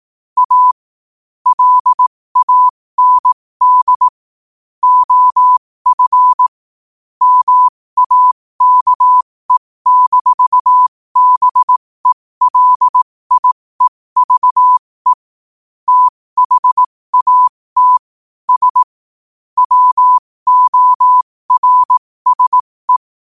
Morse code, heard as a keyed tone, about 18 wpm, spaced as for 15 wpm, 1000 Hz. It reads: A LAND OF MAKE-BELIEVE THAT S WORSE
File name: qotdfiles / 15wpm